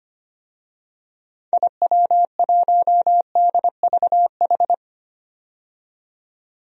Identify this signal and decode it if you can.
Morse code, heard as a keyed tone, about 25 wpm, 700 Hz. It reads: IW1DVH
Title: DXCC-100 Call Signs - 6 Characters - 25wpm